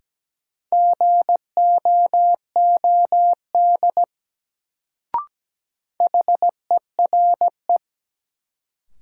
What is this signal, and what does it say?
Morse code, heard as a keyed tone, about 17 wpm, 700 Hz. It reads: GOOD HERE
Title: Sentences from Top 200 Words - New Word Drill - 17wpm